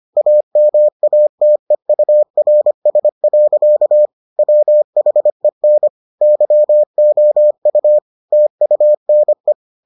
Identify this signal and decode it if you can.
Morse code, heard as a keyed tone, about 25 wpm, 600 Hz. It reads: AMATEURS. WHEN YOU TUNE